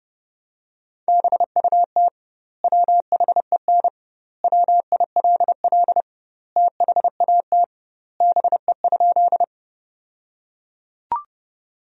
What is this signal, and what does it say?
Morse code, heard as a keyed tone, about 31 wpm, 700 Hz. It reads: BUT WHEN WILL THAT BE?
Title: 2x Word-Spacing - Sentences from Top 100 Words - 30wpm